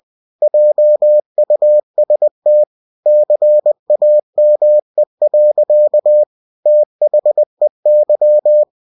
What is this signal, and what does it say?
Morse code, heard as a keyed tone, about 20 wpm, 600 Hz. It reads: JUST CAME. THEY
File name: cw-books / winnie-the-pooh / winnie-20wpm-page-0079